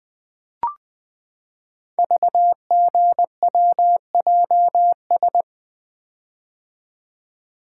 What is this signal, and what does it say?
Morse code, heard as a keyed tone, about 20 wpm, 700 Hz. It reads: VGWJS
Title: Code Groups - 5 Letters - 20wpm